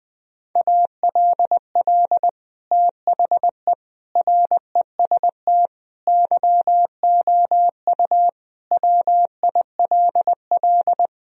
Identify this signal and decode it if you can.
Morse code, heard as a keyed tone, about 20 wpm, 700 Hz. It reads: ALL THE REST YOU WILL
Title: Sentences from Top 1000 Words - Review Part 4 - 20wpm